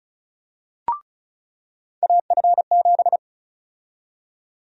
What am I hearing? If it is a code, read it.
Morse code, heard as a keyed tone, about 36 wpm, 700 Hz. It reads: AF7
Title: USA Call Sign Prefixes - 35wpm